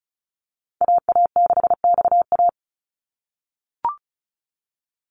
Morse code, heard as a keyed tone, about 36 wpm, 700 Hz. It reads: AA6XA
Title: Call Signs - 35wpm